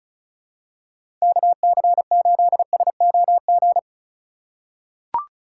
Morse code, heard as a keyed tone, about 34 wpm, 700 Hz. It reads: KC8SOG
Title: USA Call Signs - 2x Character Spacing - 35wpm